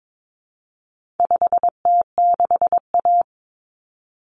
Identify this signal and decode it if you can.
Morse code, heard as a keyed tone, about 22 wpm, 700 Hz. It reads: 5T6A
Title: Common Cut Numbers - 4-Digit 22wpm